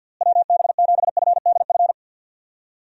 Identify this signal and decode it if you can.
Morse code, heard as a keyed tone, about 50 wpm, 700 Hz. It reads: WB7FDF
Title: USA Call Signs - 6 Characters - Repeated 2x - 50wpm